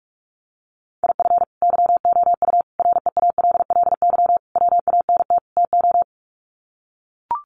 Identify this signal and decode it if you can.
Morse code, heard as a keyed tone, about 44 wpm, 700 Hz. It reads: IF YOU REALLY WANT TO
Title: Sets of 5 Words Encore2 45wpm